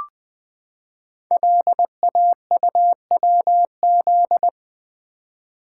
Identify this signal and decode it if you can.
Morse code, heard as a keyed tone, about 20 wpm, 700 Hz. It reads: LAUWZ
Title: Code Groups - 5 Letter:Numbers - 20wpm